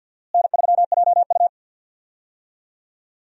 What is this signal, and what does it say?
Morse code, heard as a keyed tone, about 50 wpm, 700 Hz. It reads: N3JU
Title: USA Call Signs - 4 Characters - Repeated 3x - 50wpm